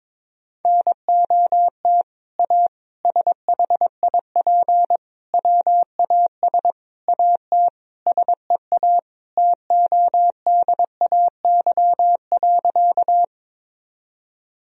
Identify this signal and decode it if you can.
Morse code, heard as a keyed tone, about 21 wpm, 700 Hz. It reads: NOT A SHIP WAS AT SEA TODAY.
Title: Sentences from Top 1000 Words - Review Part 2 - 22wpm